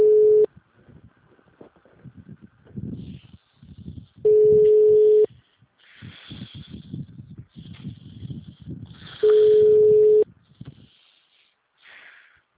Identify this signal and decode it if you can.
Ringback tone (Europe, 425 Hz).
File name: out- -2007- - - .1396